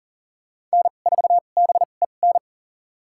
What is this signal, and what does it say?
Morse code, heard as a keyed tone, about 37 wpm, spaced as for 28 wpm, 700 Hz. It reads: N4BEN
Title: USA Call Signs - 5 Characters - 2x Character Spacing - 40wpm